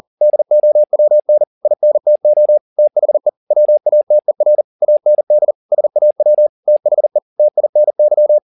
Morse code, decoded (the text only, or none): DOWN INTO THE WATER AND SAW THE TINY